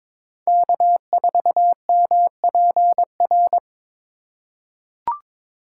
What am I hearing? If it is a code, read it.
Morse code, heard as a keyed tone, about 22 wpm, 700 Hz. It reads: K4MPR
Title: USA Call Signs - 5 Characters - 4x Character Spacing - 22wpm